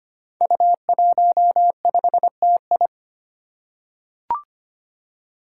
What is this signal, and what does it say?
Morse code, heard as a keyed tone, about 25 wpm, 700 Hz. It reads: U15TI
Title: Five Challenging Letter-Number Combinations 25wpm